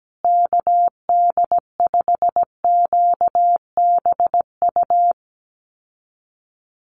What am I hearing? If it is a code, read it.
Morse code, heard as a keyed tone, about 17 wpm, 700 Hz. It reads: KD5QBU
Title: USA Call Signs Repeated 3x 17wpm